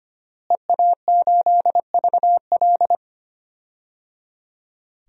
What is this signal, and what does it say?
Morse code, heard as a keyed tone, about 25 wpm, 700 Hz. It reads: EA8VL
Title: DXCC-100 Call Signs - 5 Characters - Repeated 3x - 25wpm